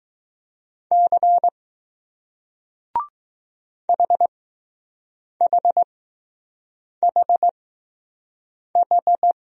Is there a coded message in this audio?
Yes: Morse code from 0.9 to 9.3 s, heard as a keyed tone, about 20 wpm, spaced as for 6 wpm, 700 Hz. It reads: C HHHH